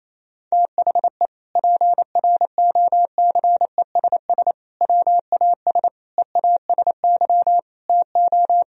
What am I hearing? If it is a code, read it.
Morse code, heard as a keyed tone, about 28 wpm, 700 Hz. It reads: THE PROCESS WAS EASY TO
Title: Sentences from Top 1000 Words - Review Part 3 - 28wpm